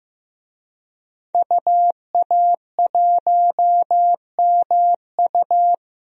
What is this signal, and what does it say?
Morse code, heard as a keyed tone, about 15 wpm, 700 Hz. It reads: UA1MU
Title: DXCC-100 Call Signs - 5 Characters - Repeated 3x - 15wpm